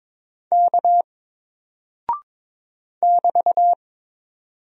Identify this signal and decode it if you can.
Morse code, heard as a keyed tone, about 22 wpm, 700 Hz. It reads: K <BT>
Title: Single Character (Letter:Number:Punctuation:Prosign) 22wpm